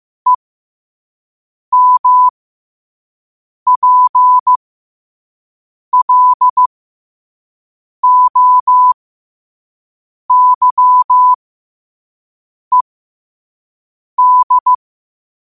Morse code, heard as a keyed tone, about 15 wpm, spaced as for 5 wpm, 1000 Hz. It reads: EMPLOYED
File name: qotdfiles / 5wpm